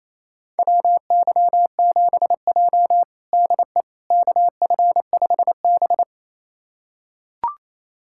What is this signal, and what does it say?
Morse code, heard as a keyed tone, about 28 wpm, 700 Hz. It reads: WY7J DE KF5B
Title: USA Callsign DE Callsign - 4 Characters - 28wpm